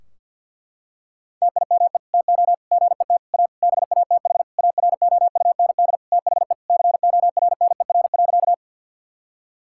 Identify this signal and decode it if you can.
Morse code, heard as a keyed tone, about 49 wpm, 700 Hz. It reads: TIME TO GET A BATH AROUND THE CORNER.
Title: Sentences from Top 1600 Words 50wpm